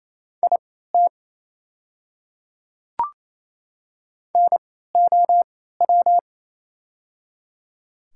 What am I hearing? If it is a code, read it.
Morse code, heard as a keyed tone, about 28 wpm, spaced as for 16 wpm, 700 Hz. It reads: IT NOW
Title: 3x Character Spacing - Top 200 Words - 28wpm